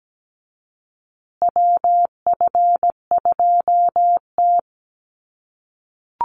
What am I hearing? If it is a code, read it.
Morse code, heard as a keyed tone, about 17 wpm, 700 Hz. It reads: WF2T
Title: USA Call Signs Repeated 2x 17wpm